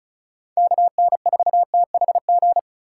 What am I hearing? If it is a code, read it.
Morse code, heard as a keyed tone, about 34 wpm, 690 Hz. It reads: KN4THG
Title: USA Call Signs - 6 Characters - 6x Character Spacing - 35wpm